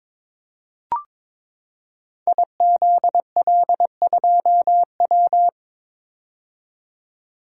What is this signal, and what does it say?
Morse code, heard as a keyed tone, about 22 wpm, 700 Hz. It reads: IZL2W
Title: Code Groups - 5 Letter:Numbers - 22wpm